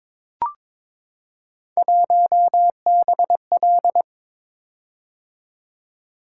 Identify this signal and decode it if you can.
Morse code, heard as a keyed tone, about 23 wpm, 700 Hz. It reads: 1BL
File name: practice / 22wpm / Three Letter-Number Combinations - 22wpm